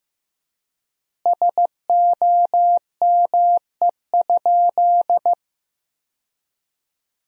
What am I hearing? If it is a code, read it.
Morse code, heard as a keyed tone, about 15 wpm, 700 Hz. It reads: SOME?
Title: 3x Word-Spacing - Sentences from Top 100 Words - 15wpm